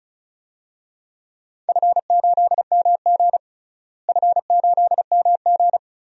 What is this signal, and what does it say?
Morse code, heard as a keyed tone, about 36 wpm, 700 Hz. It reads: F8MG F8MG
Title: DXCC-100 Call Signs - 4 Characters - Repeated 2x - 35wpm